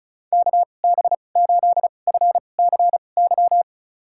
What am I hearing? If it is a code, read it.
Morse code, heard as a keyed tone, about 34 wpm, spaced as for 25 wpm, 700 Hz. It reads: KB8FCY
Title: USA Call Signs - 6 Characters - 2x Character Spacing - 35wpm